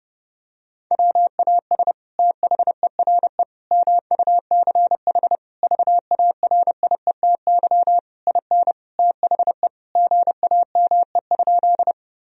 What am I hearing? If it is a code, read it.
Morse code, heard as a keyed tone, about 30 wpm, 700 Hz. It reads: WAS THERE MUCH VARIETY IN THE GAME?
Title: Sentences from Top 1000 Words - Review Part 4 - 30wpm